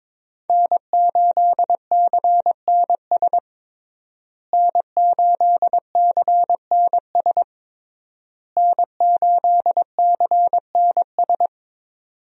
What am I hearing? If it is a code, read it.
Morse code, heard as a keyed tone, about 22 wpm, 700 Hz. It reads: N8CNS N8CNS N8CNS
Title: USA Call Signs - 5 Characters - Repeated 3x - 22wpm